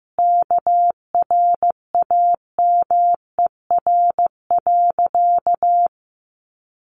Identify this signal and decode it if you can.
Morse code, heard as a keyed tone, about 15 wpm, 700 Hz. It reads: KRAMER.
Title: A Sound Of Thunder 15wpm